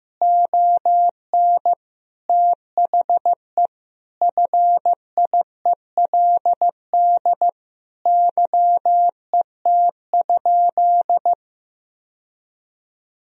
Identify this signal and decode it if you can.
Morse code, heard as a keyed tone, about 16 wpm, 700 Hz. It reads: ON THE FIELD YET?
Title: Sentences from Top 1000 Words - Review Part 2 - 15wpm